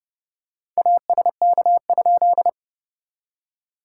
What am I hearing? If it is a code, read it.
Morse code, heard as a keyed tone, about 30 wpm, 660 Hz. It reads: ASK?
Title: SOTA QSO Practice 30wpm